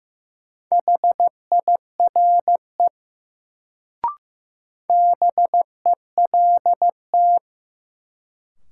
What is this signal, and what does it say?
Morse code, heard as a keyed tone, about 15 wpm, 700 Hz. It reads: HIRE BELT
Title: Sentences from Top 1400 Words - New Word Drill - 15wpm